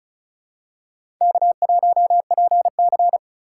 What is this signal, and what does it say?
Morse code, heard as a keyed tone, about 36 wpm, 700 Hz. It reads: K1PC
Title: USA Call Signs - 4 Characters - 3x Character Spacing - 35wpm